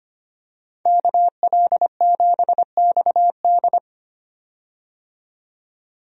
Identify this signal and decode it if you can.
Morse code, heard as a keyed tone, about 25 wpm, 700 Hz. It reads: KL7XD
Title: DXCC-100 Call Signs - 5 Characters - 25wpm